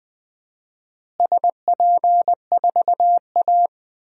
Morse code, heard as a keyed tone, about 21 wpm, 700 Hz. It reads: SP4A
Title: DXCC-100 Call Signs - 4 Characters - Repeated 3x - 20wpm